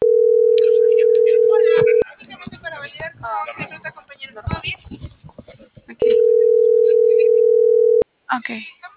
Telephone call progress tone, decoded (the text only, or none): ringback tone